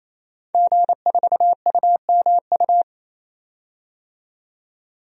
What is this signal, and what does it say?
Morse code, heard as a keyed tone, about 28 wpm, 700 Hz. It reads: G4UMU
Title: DXCC-100 Call Signs - 5 Characters - 28wpm